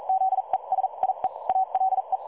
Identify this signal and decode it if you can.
Morse code, heard as a keyed tone, about 42 wpm, spaced as for 35 wpm, 760 Hz. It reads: ZE SIE ALE
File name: dlazaawansowanych / nagraniazpasma / websdr recording 3565.0kHz